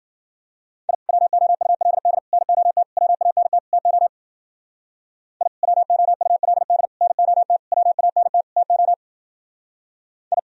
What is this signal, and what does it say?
Morse code, heard as a keyed tone, about 51 wpm, 700 Hz. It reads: I WOULD NOT WANT TO I WOULD NOT WANT TO I